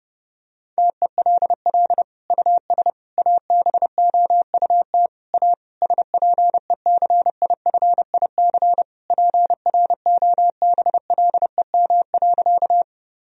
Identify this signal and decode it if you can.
Morse code, heard as a keyed tone, about 32 wpm, 700 Hz. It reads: TELL US ABOUT A SPECIFIC PROBLEM.
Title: Sentences from Top 1000 Words - Review Part 3 - 30wpm